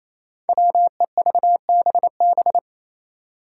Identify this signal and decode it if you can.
Morse code, heard as a keyed tone, about 28 wpm, 700 Hz. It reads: WEVBB
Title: Five Challenging Letter-Number Combinations 28wpm